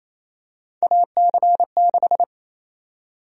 Morse code, heard as a keyed tone, about 28 wpm, 700 Hz. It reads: AC6
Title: USA Call Sign Prefixes - 28wpm